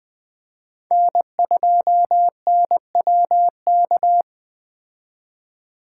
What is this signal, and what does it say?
Morse code, heard as a keyed tone, about 20 wpm, 700 Hz. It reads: N2NWK
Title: 2023 POTA Activator Call Signs - 20wpm